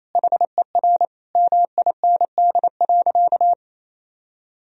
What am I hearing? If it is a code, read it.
Morse code, heard as a keyed tone, about 27 wpm, 700 Hz. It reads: HER MIND.
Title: Sentences from Top 1000 Words - Review Part 3 - 28wpm